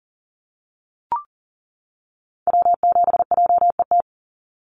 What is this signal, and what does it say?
Morse code, heard as a keyed tone, about 40 wpm, 700 Hz. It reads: W7JET
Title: NA SOTA Activator Callsigns 40wpm